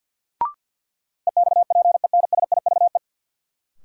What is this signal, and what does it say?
Morse code, heard as a keyed tone, about 53 wpm, 700 Hz. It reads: EXPENSIVE
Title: Sentences from Top 1000 Words - Review Part 6 - New Word Drill - 50wpm